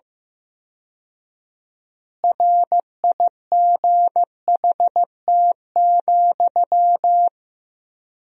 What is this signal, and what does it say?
Morse code, heard as a keyed tone, about 15 wpm, 700 Hz. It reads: RIGHT,
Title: 4x Word-Spacing - Sentences from Top 400 Words - 15wpm